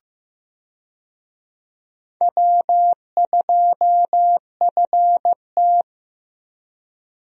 Morse code, heard as a keyed tone, about 16 wpm, 700 Hz. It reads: W2FT
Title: USA Call Signs - 4 Characters - Repeated 3x - 15wpm